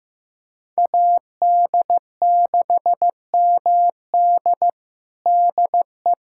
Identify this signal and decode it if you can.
Morse code, heard as a keyed tone, about 16 wpm, 700 Hz. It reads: AD6MD DE
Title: USA Callsign DE Callsign - 5 Characters - 15wpm